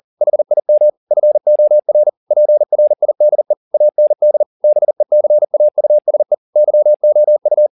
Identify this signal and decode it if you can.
Morse code, heard as a keyed tone, about 40 wpm, 600 Hz. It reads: HIM FOR PRIDE AND BECAUSE YOU
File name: the old man and the sea 2 40wpm